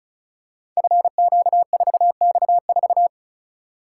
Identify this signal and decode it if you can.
Morse code, heard as a keyed tone, about 36 wpm, 700 Hz. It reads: FQ4X4
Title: Code Groups - 5 Letter:Numbers - 35wpm